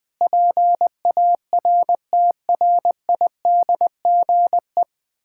Morse code, heard as a keyed tone, about 21 wpm, 700 Hz. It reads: PARTRIDGE
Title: Sentences with 9-Word Phrases - 20wpm